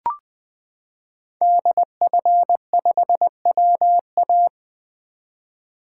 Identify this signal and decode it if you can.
Morse code, heard as a keyed tone, about 20 wpm, 700 Hz. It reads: DF5WA
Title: 2023 SOTA Chaser Call Signs - 20wpm